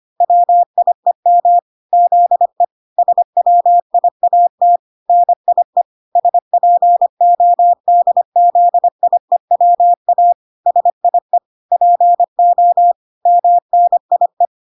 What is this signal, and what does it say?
Morse code, heard as a keyed tone, about 25 wpm, 700 Hz. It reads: WIEM ZE SWIAT NIE SPODZIEWA SIE PO MNIE